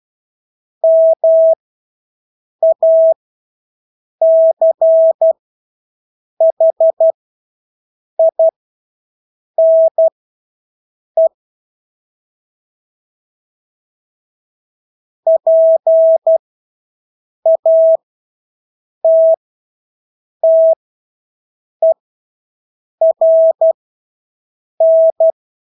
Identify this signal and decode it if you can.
Morse code, heard as a keyed tone, about 12 wpm, spaced as for 6 wpm, 650 Hz. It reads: MACHINE PATTERN
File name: Engelske ord 12-6 wpm